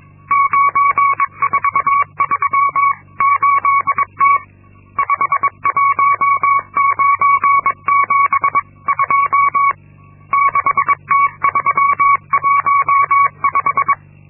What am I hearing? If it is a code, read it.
Morse code, heard as a keyed tone, about 22 wpm, 1120 Hz. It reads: 9438T 51972 6T315
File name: Linksmakalnis-2